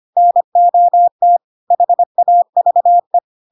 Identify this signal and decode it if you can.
Morse code, heard as a keyed tone, about 25 wpm, 700 Hz. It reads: NOT HAVE
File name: Pinocchio / Pinocchio ch3 25 25 ik1hge